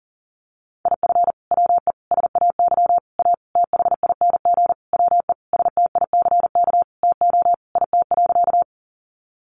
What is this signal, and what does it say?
Morse code, heard as a keyed tone, about 40 wpm, 700 Hz. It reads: IF WE SAY A THING WE STICK TO IT.
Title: Sentences from Top 1200 Words 40wpm